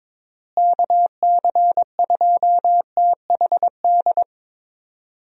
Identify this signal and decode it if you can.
Morse code, heard as a keyed tone, about 22 wpm, 700 Hz. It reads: KC2THD
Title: USA Call Signs - 6 Characters - Repeated 3x - 22wpm